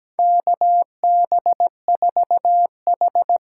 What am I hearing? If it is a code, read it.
Morse code, heard as a keyed tone, about 17 wpm, 700 Hz. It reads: KB4H
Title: USA Call Signs - 6 Characters - 8x Character Spacing - 17wpm